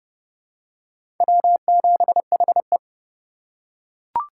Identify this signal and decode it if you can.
Morse code, heard as a keyed tone, about 31 wpm, 700 Hz. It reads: W7HE
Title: USA Call Signs - 4 Characters - 6x Character Spacing - 30wpm